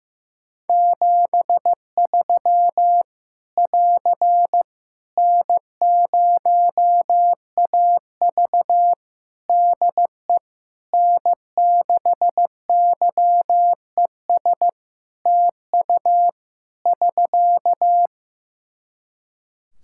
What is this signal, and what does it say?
Morse code, heard as a keyed tone, about 15 wpm, 700 Hz. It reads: 73 <AR> N0AV DE N6YES TU <SK>
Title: Practice QSO - Exchange 4 - Ending - 15wpm